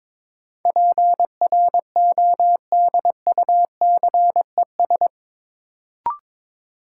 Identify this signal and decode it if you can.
Morse code, heard as a keyed tone, about 22 wpm, 700 Hz. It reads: PRODUCES
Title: Sentences from Top 1000 Words - Review Part 6 - New Word Drill - 22wpm